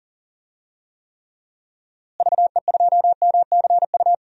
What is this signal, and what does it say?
Morse code, heard as a keyed tone, about 37 wpm, 700 Hz. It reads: VE2MCU
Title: DXCC-100 Call Signs - 6 Characters - Repeated 3x - 40wpm